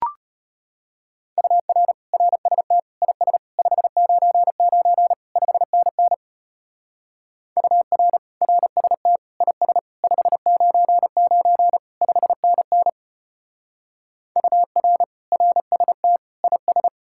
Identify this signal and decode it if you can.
Morse code, heard as a keyed tone, about 32 wpm, 700 Hz. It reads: UR RST IS 599 5NN UR RST IS 599 5NN UR RST IS